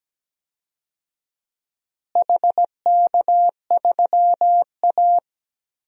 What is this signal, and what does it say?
Morse code, heard as a keyed tone, about 17 wpm, 700 Hz. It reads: HK3A
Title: DXCC-100 Call Signs - 4 Characters - Repeated 3x - 17wpm